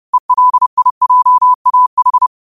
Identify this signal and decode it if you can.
Morse code, heard as a keyed tone, about 30 wpm, 1000 Hz. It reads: ELIJAH
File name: qotdfiles / 30wpm